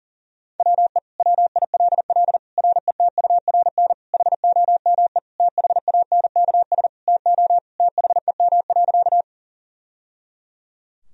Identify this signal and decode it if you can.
Morse code, heard as a keyed tone, about 40 wpm, 700 Hz. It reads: WE WILL RETURN HOME THANKS TO THEM.
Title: Sentences from Top 1000 Words - Review Part 3 - 40wpm